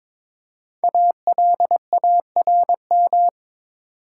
Morse code, heard as a keyed tone, about 22 wpm, 700 Hz. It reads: ALARM
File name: 5 Letter Words - First 3 Letters Given - 22wpm